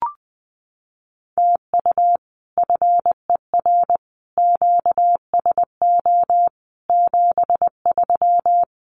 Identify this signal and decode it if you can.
Morse code, heard as a keyed tone, about 20 wpm, 700 Hz. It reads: TU FER QSO 73
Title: Common QSO Phrases Encore 20wpm